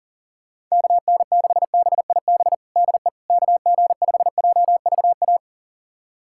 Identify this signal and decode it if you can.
Morse code, heard as a keyed tone, about 40 wpm, 700 Hz. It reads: KN6BIB DE KG5JVA
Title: USA Callsign DE Callsign - 6 Characters - 40wpm